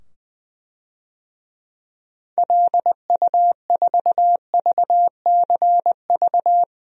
Morse code, heard as a keyed tone, about 20 wpm, 700 Hz. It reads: LU4VCV
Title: DXCC-100 Call Signs - 6 Characters - 20wpm